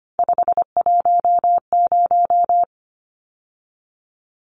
Morse code, heard as a keyed tone, about 25 wpm, 700 Hz. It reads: 510
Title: Numbers - 3-Digit 25wpm